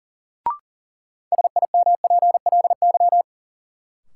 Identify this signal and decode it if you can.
Morse code, heard as a keyed tone, about 37 wpm, 700 Hz. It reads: SIMPLY
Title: Sentences from Top 600 Words - New Word Drill - 40wpm